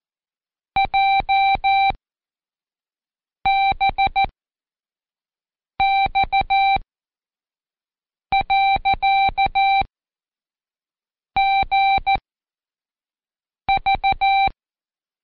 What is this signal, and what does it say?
Morse code, heard as a keyed tone, about 14 wpm, 780 Hz. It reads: J B X . G V
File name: Code Quick / AudioDownld1 / Seg69-Day8-CheckTest 8